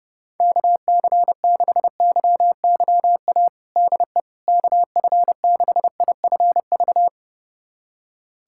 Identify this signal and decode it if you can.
Morse code, heard as a keyed tone, about 30 wpm, 700 Hz. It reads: KC6YYA DE KF6IFV
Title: USA Callsign DE Callsign - 6 Characters - 30wpm